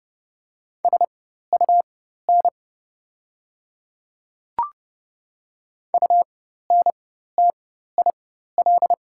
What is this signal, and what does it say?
Morse code, heard as a keyed tone, about 30 wpm, spaced as for 14 wpm, 700 Hz. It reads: SUN UNTIL
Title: 4x Character Spacing - Top 300 Words - 30wpm